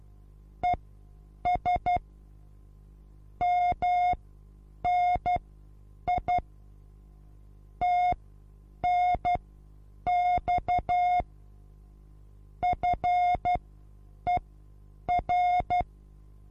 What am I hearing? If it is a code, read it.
Morse code, heard as a keyed tone, about 12 wpm, spaced as for 8 wpm, 720 Hz. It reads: ES MNI TNX FER
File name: repos / Morsekurs / DJ4UF Morsen23